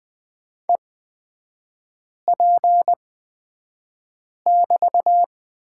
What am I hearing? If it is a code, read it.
Morse code, heard as a keyed tone, about 20 wpm, spaced as for 5 wpm, 700 Hz. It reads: EP<BT>
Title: 250 Lesson 40e - BK - Three Characters 20wpm